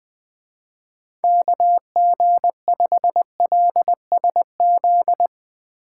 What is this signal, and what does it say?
Morse code, heard as a keyed tone, about 20 wpm, 700 Hz. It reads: KG5LSZ